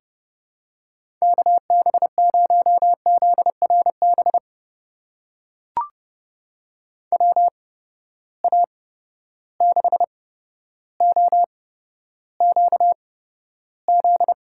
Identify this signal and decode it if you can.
Morse code, heard as a keyed tone, about 30 wpm, spaced as for 13 wpm, 700 Hz. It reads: KB0ZRB WA6OQZ
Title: USA Call Signs - 6 Characters - 8x Character Spacing - 30wpm